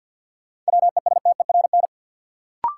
Morse code, heard as a keyed tone, about 52 wpm, 700 Hz. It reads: WESTERN
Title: Sentences from Top 1000 Words - Review Part 7 - New Word Drill - 50wpm